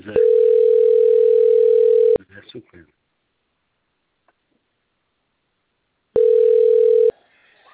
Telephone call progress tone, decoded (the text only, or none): ringback tone